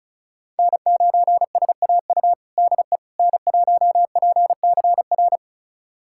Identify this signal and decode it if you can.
Morse code, heard as a keyed tone, about 36 wpm, 700 Hz. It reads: N9SAU DE N1PCR
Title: USA Callsign DE Callsign - 5 Characters - 35wpm